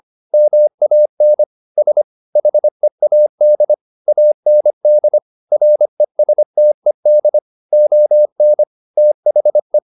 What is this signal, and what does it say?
Morse code, heard as a keyed tone, about 25 wpm, 600 Hz. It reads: MAN S HEAD AND RESTED ON THE
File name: the old man and the sea part 1 25wpm